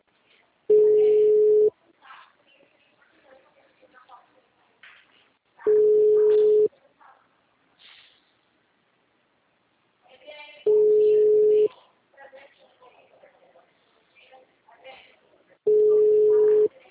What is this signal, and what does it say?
Ringback tone (Europe, 425 Hz), with speech.
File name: out- -2008- - - .46140